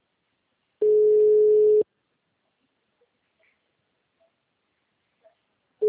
Ringback tone (Europe, 425 Hz).